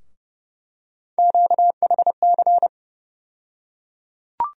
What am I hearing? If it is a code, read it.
Morse code, heard as a keyed tone, about 31 wpm, 700 Hz. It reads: QHC
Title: USA Call Sign Postfixes - 3 Letters 30wpm